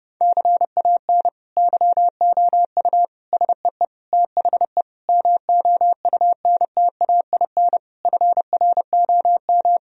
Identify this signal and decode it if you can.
Morse code, heard as a keyed tone, about 30 wpm, 700 Hz. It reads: CAN YOU SEE THE MOUNTAIN FROM
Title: Sentences from Top 1000 Words - Review Part 1 - 30wpm